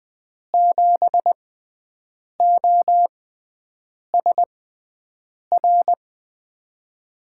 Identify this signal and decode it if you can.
Morse code, heard as a keyed tone, about 20 wpm, spaced as for 7 wpm, 700 Hz. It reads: 7OSR
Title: USA Call Signs - 6 Characters - 6x Character Spacing - 20wpm